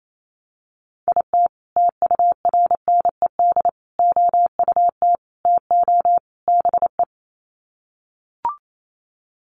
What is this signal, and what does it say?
Morse code, heard as a keyed tone, about 28 wpm, 700 Hz. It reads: IT TURNED OUT TO BE
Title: Sets of 5 Words 28wpm